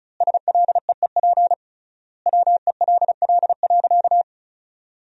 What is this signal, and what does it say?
Morse code, heard as a keyed tone, about 34 wpm, 700 Hz. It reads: SLEEP WELL.
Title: 3x Word-Spacing - Sentences from Top 500 Words - 35wpm